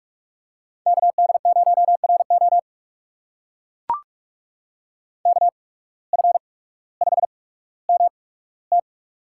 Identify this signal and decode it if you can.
Morse code, heard as a keyed tone, about 44 wpm, spaced as for 19 wpm, 700 Hz. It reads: KD0RO KF5MT
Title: USA Call Signs - 5 Characters - 8x Character Spacing - 45wpm